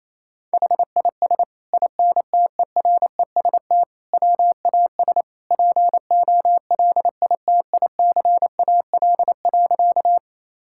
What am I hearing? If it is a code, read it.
Morse code, heard as a keyed tone, about 28 wpm, 700 Hz. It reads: HIS INTEREST WAS POLITICAL.
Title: Sentences from Top 1000 Words - Review Part 5 - 28wpm